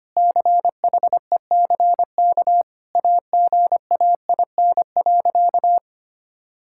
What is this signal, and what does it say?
Morse code, heard as a keyed tone, about 25 wpm, 700 Hz. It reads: CHECK AGAIN.
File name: practice / 25wpm / Sentences with 4-Word Phrases - 25wpm